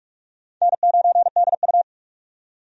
Morse code, heard as a keyed tone, about 47 wpm, 700 Hz. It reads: N9DU
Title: USA Call Signs - 4 Characters - 45wpm